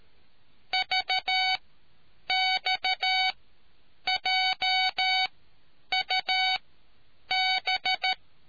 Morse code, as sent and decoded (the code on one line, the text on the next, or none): ...- -..- .--- ..- -...
VXJUB